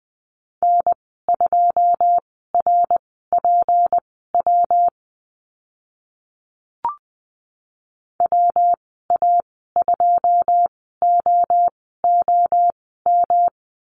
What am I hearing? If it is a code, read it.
Morse code, heard as a keyed tone, about 20 wpm, spaced as for 15 wpm, 700 Hz. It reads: N2RPW WA2OOM
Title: 161 Lesson 26g - Letter g - Call Signs 20wpm